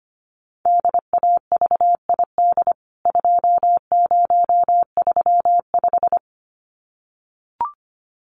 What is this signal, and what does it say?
Morse code, heard as a keyed tone, about 25 wpm, 690 Hz. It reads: DAVID 2035
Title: CWT Exchange 25wpm